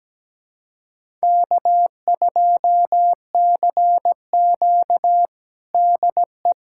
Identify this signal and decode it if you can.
Morse code, heard as a keyed tone, about 17 wpm, 700 Hz. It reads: K2CQ DE
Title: USA Callsign DE Callsign - 4 Characters - 17wpm